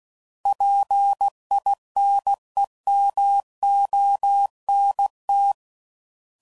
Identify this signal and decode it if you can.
Morse code, heard as a keyed tone, about 16 wpm, 800 Hz. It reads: PINEMONT
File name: CW Code Groups @15WPM